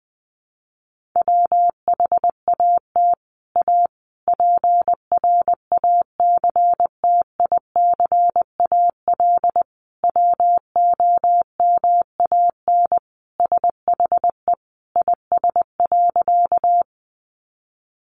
Morse code, heard as a keyed tone, about 20 wpm, 700 Hz. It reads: WHAT A PRACTICAL WOMAN SHE IS.
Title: Sentences from Top 1400 Words 20wpm